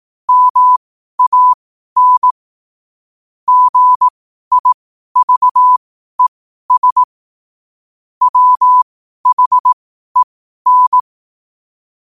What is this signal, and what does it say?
Morse code, heard as a keyed tone, about 18 wpm, spaced as for 12 wpm, 1000 Hz. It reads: MAN GIVES WHEN
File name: qotdfiles / 12wpm